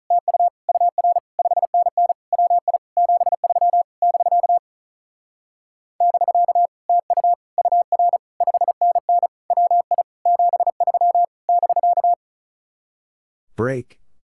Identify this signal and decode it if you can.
Morse code, heard as a keyed tone, about 43 wpm, 700 Hz. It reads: TU UR 5NN WI 73 <BK> <BK> TU UR 5NN WI 73 <BK>
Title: Speed-Racing Common QSO Phrases - 35wpm